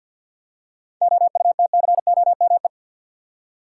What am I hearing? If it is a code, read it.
Morse code, heard as a keyed tone, about 47 wpm, 700 Hz. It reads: OUTCOME
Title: Seven Letter Words - 50wpm